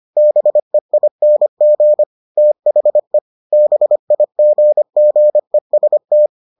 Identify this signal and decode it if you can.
Morse code, heard as a keyed tone, about 25 wpm, 600 Hz. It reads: BEING THE BIGGEST